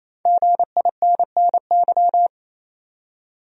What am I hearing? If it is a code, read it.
Morse code, heard as a keyed tone, about 27 wpm, 700 Hz. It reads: GINNY
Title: USA Common Names - Max 5 Letters - 28wpm